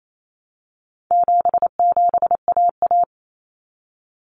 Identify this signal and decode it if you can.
Morse code, heard as a keyed tone, about 28 wpm, 700 Hz. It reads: 77AA
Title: Common Cut Numbers - 4-Digit 28wpm